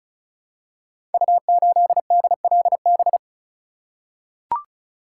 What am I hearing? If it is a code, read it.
Morse code, heard as a keyed tone, about 36 wpm, 700 Hz. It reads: U8DLB
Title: Code Groups - 5 Letter:Numbers - 35wpm